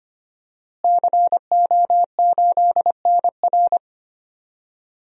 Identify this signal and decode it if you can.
Morse code, heard as a keyed tone, about 25 wpm, 700 Hz. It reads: CO8NR
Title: DXCC-100 Call Signs - 5 Characters - Repeated 3x - 25wpm